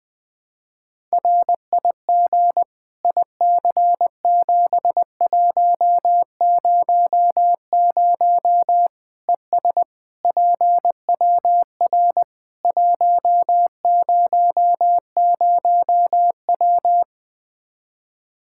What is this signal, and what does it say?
Morse code, heard as a keyed tone, about 20 wpm, 700 Hz. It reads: RIG IC7100 ES PWR 100W
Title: Practice QSO - Exchange 2 - Info - 20wpm